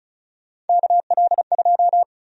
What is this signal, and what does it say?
Morse code, heard as a keyed tone, about 36 wpm, 700 Hz. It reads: KL2
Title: USA Call Sign Prefixes - 35wpm